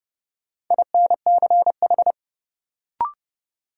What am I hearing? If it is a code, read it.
Morse code, heard as a keyed tone, about 30 wpm, 700 Hz. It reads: INCH
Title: Sentences from Top 1000 Words - Review Part 3 - New Word Drill - 30wpm